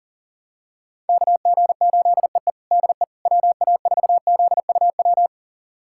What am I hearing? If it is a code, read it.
Morse code, heard as a keyed tone, about 40 wpm, 700 Hz. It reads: KG8EE DE WA4ZUW
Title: USA Callsign DE Callsign 40wpm